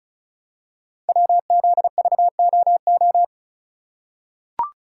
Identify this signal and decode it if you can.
Morse code, heard as a keyed tone, about 36 wpm, 700 Hz. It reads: WZVOO
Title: Code Groups - 5 Letter:Numbers - 35wpm